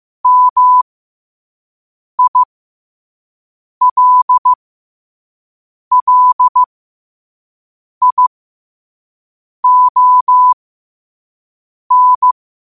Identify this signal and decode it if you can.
Morse code, heard as a keyed tone, about 15 wpm, spaced as for 5 wpm, 1000 Hz. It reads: MILLION
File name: qotdfiles / 5wpm